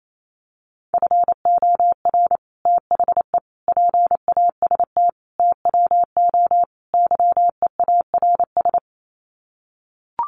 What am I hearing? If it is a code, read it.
Morse code, heard as a keyed tone, about 28 wpm, 700 Hz. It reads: FOR THE PAST TWO YEARS
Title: Sets of 5 Words Encore2 28wpm